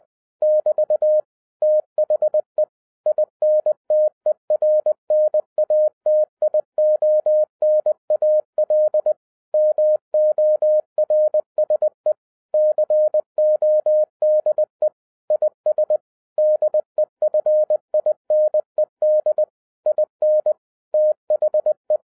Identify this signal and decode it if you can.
Morse code, heard as a keyed tone, about 20 wpm, 610 Hz. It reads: <BT> THE INTERNATIONAL MORSE CODE IS DEFINED IN THE